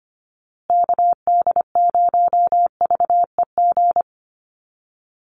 Morse code, heard as a keyed tone, about 25 wpm, 700 Hz. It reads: KD0VEG
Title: USA Call Signs Repeated 2x 25wpm